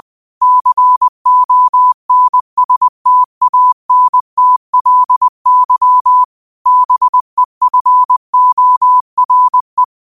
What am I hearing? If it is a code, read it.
Morse code, heard as a keyed tone, about 20 wpm, 1000 Hz. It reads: CONSTANTLY BEFORE
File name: qotdfiles / 20wpm